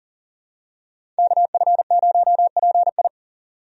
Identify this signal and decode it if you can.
Morse code, heard as a keyed tone, about 40 wpm, 700 Hz. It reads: KF0PI
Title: USA Call Signs - 5 Characters - 40wpm